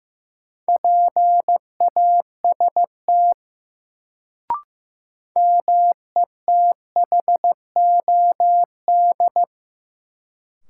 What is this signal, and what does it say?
Morse code, heard as a keyed tone, about 15 wpm, 700 Hz. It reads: PAST METHOD
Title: Sentences from Top 600 Words - New Word Drill - 15wpm